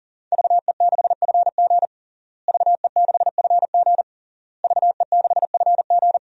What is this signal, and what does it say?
Morse code, heard as a keyed tone, about 40 wpm, 700 Hz. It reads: VE6FG VE6FG VE6FG
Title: DXCC-100 Call Signs - 5 Characters - Repeated 3x - 40wpm